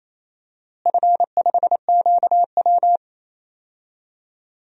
Morse code, heard as a keyed tone, about 27 wpm, 700 Hz. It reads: F5QW
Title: DXCC-100 Call Signs - 4 Characters - Repeated 3x - 28wpm